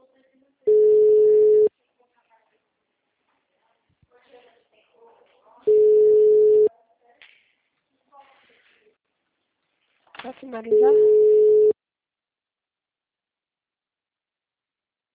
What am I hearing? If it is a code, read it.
Ringback tone (Europe, 425 Hz), with speech.
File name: out- -2007- - - .4854